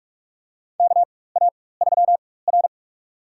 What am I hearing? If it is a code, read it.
Morse code, heard as a keyed tone, about 45 wpm, spaced as for 21 wpm, 690 Hz. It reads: KA3R
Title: USA Call Signs - 4 Characters - 4x Character Spacing - 45wpm